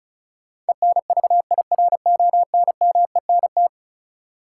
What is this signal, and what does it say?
Morse code, heard as a keyed tone, about 36 wpm, 700 Hz. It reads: ENVIRONMENT
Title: Sentences from Top 1000 Words - Review Part 5 - New Word Drill - 35wpm